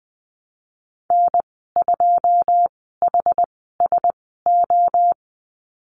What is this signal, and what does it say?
Morse code, heard as a keyed tone, about 20 wpm, spaced as for 14 wpm, 700 Hz. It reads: N2HSO
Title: 082 Lesson 14g - Number 2 - Call Signs 20wpm